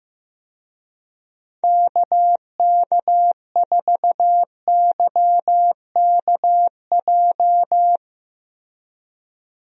Morse code, heard as a keyed tone, about 15 wpm, 700 Hz. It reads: KK4YKJ
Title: USA Call Signs - 6 Characters - Repeated 2x - 15wpm